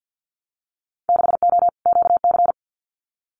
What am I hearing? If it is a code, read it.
Morse code, heard as a keyed tone, about 50 wpm, 700 Hz. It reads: 6O QC
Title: Winter Field Day Exchanges Encore 50wpm